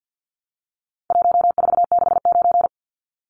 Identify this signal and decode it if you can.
Morse code, heard as a keyed tone, about 48 wpm, 700 Hz. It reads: 1469
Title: Numbers - 4-Digit 50wpm